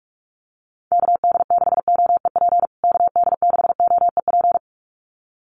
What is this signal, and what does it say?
Morse code, heard as a keyed tone, about 44 wpm, 700 Hz. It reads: KD6OEP KD6OEP
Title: USA Call Signs Repeated 2x 45wpm